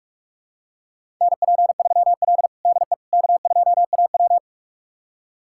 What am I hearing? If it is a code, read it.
Morse code, heard as a keyed tone, about 46 wpm, 700 Hz. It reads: NP3L DE K2AW
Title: USA Callsign DE Callsign - 4 Characters - 45wpm